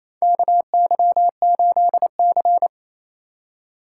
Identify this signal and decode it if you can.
Morse code, heard as a keyed tone, about 28 wpm, 700 Hz. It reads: KY8C
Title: USA Call Signs - 4 Characters - 4x Character Spacing - 28wpm